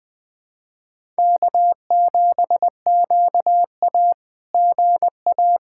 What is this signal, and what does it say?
Morse code, heard as a keyed tone, about 20 wpm, 700 Hz. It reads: K7QA GA
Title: SOTA QSO Practice 20wpm